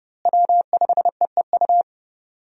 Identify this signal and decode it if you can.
Morse code, heard as a keyed tone, about 30 wpm, 700 Hz. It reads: W5EEU
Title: USA Call Signs - 5 Characters - Repeated 2x - 30wpm